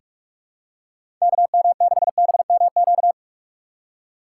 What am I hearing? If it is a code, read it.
Morse code, heard as a keyed tone, about 43 wpm, 700 Hz. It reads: KM6BMQ
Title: USA Call Signs - 6 Characters - 3x Character Spacing - 45wpm